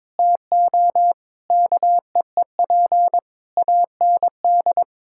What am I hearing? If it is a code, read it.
Morse code, heard as a keyed tone, about 22 wpm, 700 Hz. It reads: TO KEEP AND
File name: practice / 22wpm / Sets of 8 Words - 22wpm